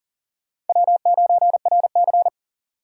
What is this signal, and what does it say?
Morse code, heard as a keyed tone, about 51 wpm, 700 Hz. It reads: W9RC